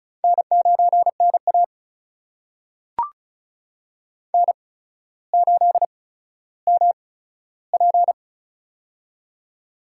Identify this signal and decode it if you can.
Morse code, heard as a keyed tone, about 36 wpm, spaced as for 15 wpm, 700 Hz. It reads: N9NA N8MP
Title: USA Call Signs - 4 Characters - 8x Character Spacing - 35wpm